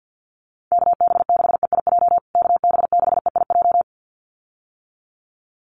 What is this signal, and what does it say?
Morse code, heard as a keyed tone, about 50 wpm, 700 Hz. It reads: KB6EIJ KB6EIJ
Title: USA Call Signs Repeated 2x 50wpm